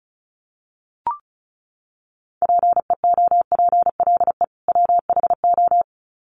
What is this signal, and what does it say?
Morse code, heard as a keyed tone, about 34 wpm, 700 Hz. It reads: PEOPLE WHO
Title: Sets of 2 Words 35wpm